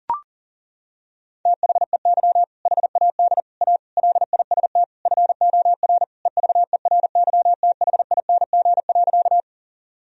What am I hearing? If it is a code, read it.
Morse code, heard as a keyed tone, about 40 wpm, 700 Hz. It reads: THEY HAD A LIST FOR EVERYTHING.
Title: Sentences from Top 1000 Words - Review Part 2 - 40wpm